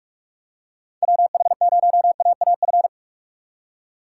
Morse code, heard as a keyed tone, about 44 wpm, 700 Hz. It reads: WH0AAF